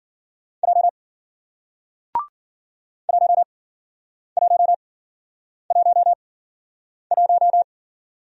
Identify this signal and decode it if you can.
Morse code, heard as a keyed tone, about 54 wpm, spaced as for 9 wpm, 700 Hz. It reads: J 1111